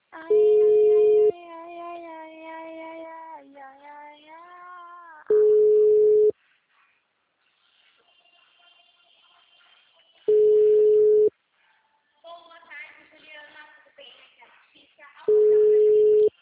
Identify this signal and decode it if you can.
Ringback tone (Europe, 425 Hz), with speech.